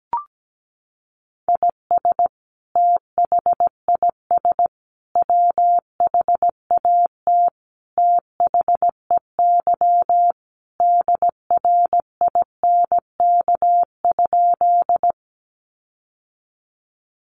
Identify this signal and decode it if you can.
Morse code, heard as a keyed tone, about 17 wpm, 700 Hz. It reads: IS THIS WHAT THEY DRINK?
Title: Sentences from Top 1500 Words 17wpm